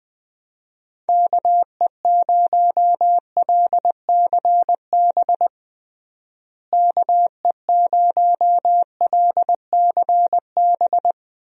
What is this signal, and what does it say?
Morse code, heard as a keyed tone, about 20 wpm, 700 Hz. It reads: KE0LCB KE0LCB
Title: USA Call Signs - 6 Characters - Repeated 3x - 20wpm